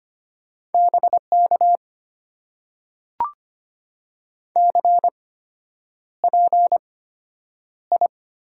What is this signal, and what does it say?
Morse code, heard as a keyed tone, about 25 wpm, spaced as for 9 wpm, 690 Hz. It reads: BK CPI
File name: Top 90 QSO Elements - 8x Character Spacing - 25wpm